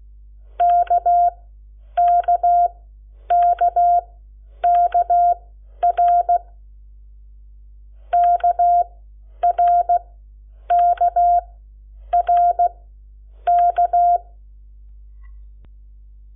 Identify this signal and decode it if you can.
Morse code, heard as a keyed tone, about 16 wpm, spaced as for 9 wpm, 700 Hz. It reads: KKKKR KRKRK
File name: Lecciones / CLASE Nº7